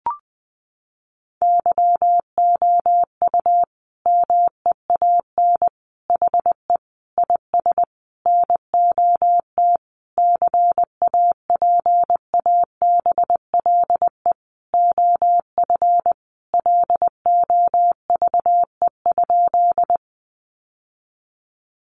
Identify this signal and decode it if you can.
Morse code, heard as a keyed tone, about 19 wpm, 700 Hz. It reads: YOU MEAN HE IS NOT CAPABLE OF LOVE?
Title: Sentences from Top 1200 Words 20wpm